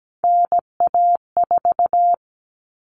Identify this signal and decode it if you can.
Morse code, heard as a keyed tone, about 17 wpm, 700 Hz. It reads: NA4
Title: All Cut Numbers - 3-Digit 17wpm